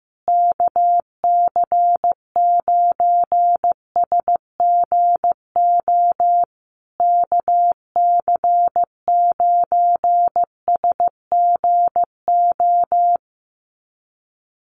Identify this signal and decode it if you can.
Morse code, heard as a keyed tone, about 15 wpm, 700 Hz. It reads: KC9SGO KC9SGO
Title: USA Call Signs Repeated 3x 15wpm